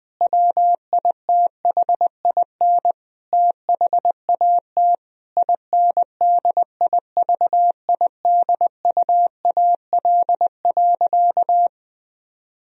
Morse code, heard as a keyed tone, about 20 wpm, 700 Hz. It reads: WITHIN THAT INDIVIDUAL.
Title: Sentences from Top 1000 Words - Review Part 4 - 20wpm